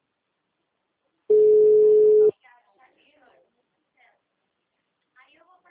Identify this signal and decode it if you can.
Ringback tone (Europe, 425 Hz), with speech.
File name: out- -2009- - - .74329